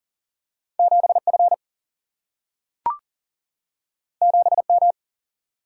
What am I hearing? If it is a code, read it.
Morse code, heard as a keyed tone, about 40 wpm, 700 Hz. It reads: 7 F 7 M
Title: Single-Letter Single-Number 40wpm